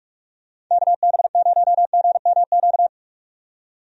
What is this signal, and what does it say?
Morse code, heard as a keyed tone, about 46 wpm, 700 Hz. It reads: KB0GMQ
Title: USA Call Signs - 6 Characters - 45wpm